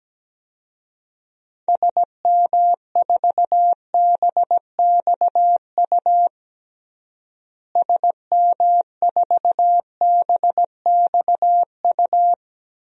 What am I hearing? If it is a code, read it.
Morse code, heard as a keyed tone, about 17 wpm, 700 Hz. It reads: SM4BXU SM4BXU
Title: DXCC-100 Call Signs - 6 Characters - Repeated 3x - 17wpm